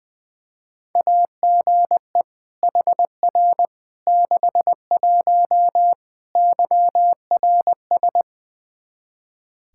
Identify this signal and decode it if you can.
Morse code, heard as a keyed tone, about 20 wpm, 700 Hz. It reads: AGE HR 61 YRS
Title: Practice QSO - Exchange 3 - More Info - 20wpm